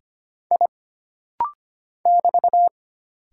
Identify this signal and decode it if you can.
Morse code, heard as a keyed tone, about 25 wpm, 700 Hz. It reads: I <BT>
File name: Single Character (Letter:Number:Punctuation:Prosign) - Rapid-Fire - 25wpm